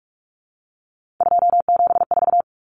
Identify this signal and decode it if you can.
Morse code, heard as a keyed tone, about 44 wpm, 700 Hz. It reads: 274
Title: Numbers - 3-Digit 45wpm